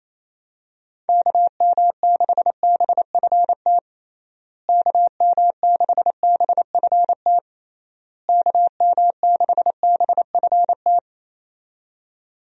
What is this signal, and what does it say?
Morse code, heard as a keyed tone, about 28 wpm, 700 Hz. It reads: KM6BFT KM6BFT KM6BFT
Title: USA Call Signs - 6 Characters - Repeated 3x - 28wpm